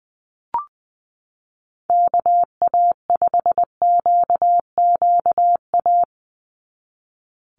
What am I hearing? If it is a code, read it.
Morse code, heard as a keyed tone, about 20 wpm, 690 Hz. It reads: KA5QQA
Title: NA SOTA Chaser Callsigns 20wpm